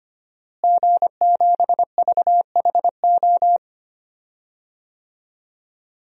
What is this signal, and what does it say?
Morse code, heard as a keyed tone, about 25 wpm, 700 Hz. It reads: G7VHO
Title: DXCC-100 Call Signs - 5 Characters - Repeated 2x - 25wpm